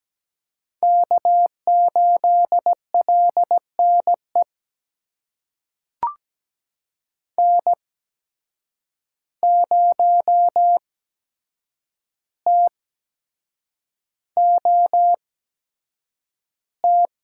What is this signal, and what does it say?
Morse code, heard as a keyed tone, about 17 wpm, spaced as for 7 wpm, 700 Hz. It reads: K8LNE N0TOT
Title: USA Call Signs - 5 Characters - 8x Character Spacing - 17wpm